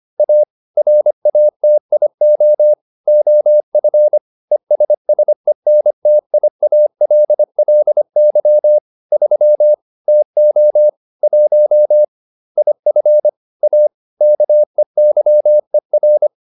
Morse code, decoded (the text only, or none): A RATIO OF ESSENTIALLY 3 TO 1 IF A KEYER